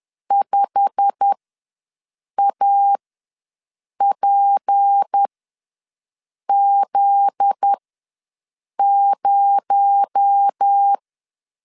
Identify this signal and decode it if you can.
Morse code, heard as a keyed tone, about 10 wpm, spaced as for 6 wpm, 800 Hz. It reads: 5APZ0
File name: LEZIONI / 10 SIGNS / 10A SLASH ws 10wpm 4121